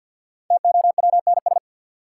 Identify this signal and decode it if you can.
Morse code, heard as a keyed tone, about 52 wpm, 700 Hz. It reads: TOWNS
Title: Sentences from Top 1000 Words - Review Part 6 - New Word Drill - 50wpm